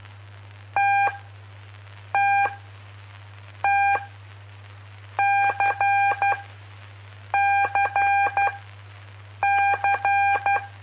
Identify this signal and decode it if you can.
Morse code, heard as a keyed tone, about 12 wpm, spaced as for 6 wpm, 800 Hz. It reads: TTT CCC
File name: pp5bar / aulas de telegrafia / Aula-05